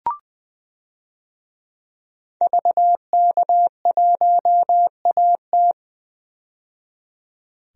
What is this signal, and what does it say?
Morse code, heard as a keyed tone, about 20 wpm, 700 Hz. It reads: VK1AT
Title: DXCC-100 Call Signs - 5 Characters - 20wpm